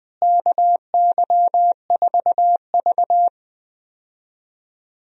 Morse code, heard as a keyed tone, about 19 wpm, 700 Hz. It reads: KY4V
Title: USA Call Signs - 4 Characters - Repeated 2x - 20wpm